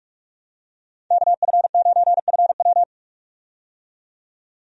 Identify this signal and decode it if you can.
Morse code, heard as a keyed tone, about 46 wpm, 700 Hz. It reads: KF9FW
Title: USA Call Signs - 5 Characters - 45wpm